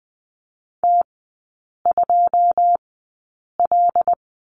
Morse code, heard as a keyed tone, about 20 wpm, spaced as for 8 wpm, 690 Hz. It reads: T2L
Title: 087 Lesson 15e - Number 5 - Three Characters 20wpm